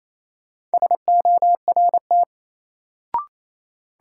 Morse code, heard as a keyed tone, about 28 wpm, 700 Hz. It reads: SORT
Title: Sets of 2 Words - New Word Drill - 28wpm